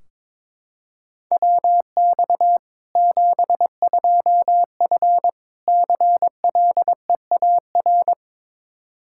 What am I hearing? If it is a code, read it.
Morse code, heard as a keyed tone, about 22 wpm, 700 Hz. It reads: WX 72F CLEAR
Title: Practice QSO - Exchange 2 - Info - 22wpm